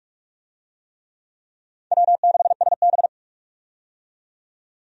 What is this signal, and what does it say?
Morse code, heard as a keyed tone, about 46 wpm, 700 Hz. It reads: W6SB
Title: USA Call Signs - 4 Characters - 45wpm